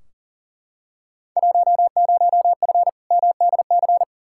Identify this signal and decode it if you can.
Morse code, heard as a keyed tone, about 40 wpm, 700 Hz. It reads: 10F MDC
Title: Field Day Exchanges Encore - 40wpm